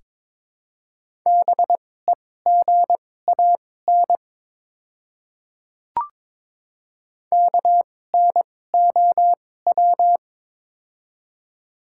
Morse code, heard as a keyed tone, about 22 wpm, spaced as for 16 wpm, 700 Hz. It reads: BEGAN KNOW
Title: 2x Character Spacing - Top 500 Words - 22wpm